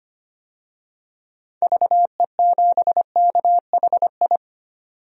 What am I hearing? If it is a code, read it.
Morse code, heard as a keyed tone, about 25 wpm, 700 Hz. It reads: VE7KHI
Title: DXCC-100 Call Signs - 6 Characters - 25wpm